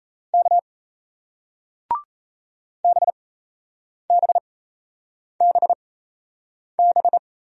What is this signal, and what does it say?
Morse code, heard as a keyed tone, about 38 wpm, spaced as for 8 wpm, 700 Hz. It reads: K BBBB